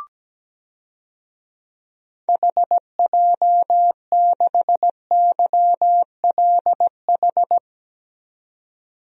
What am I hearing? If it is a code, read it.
Morse code, heard as a keyed tone, about 17 wpm, 700 Hz. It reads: HJ6YLH
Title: DXCC-100 Call Signs - 6 Characters - 17wpm